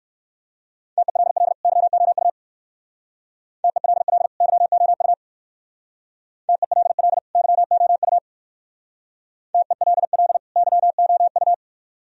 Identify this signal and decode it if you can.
Morse code, heard as a keyed tone, about 56 wpm, 690 Hz. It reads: TELL YOU TELL YOU TELL YOU TELL YOU
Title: Speed-Racing - Sets of 2 Words 45wpm